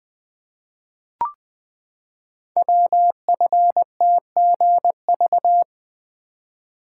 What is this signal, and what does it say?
Morse code, heard as a keyed tone, about 20 wpm, 700 Hz. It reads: WFTGV
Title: Code Groups - 5 Letters - 20wpm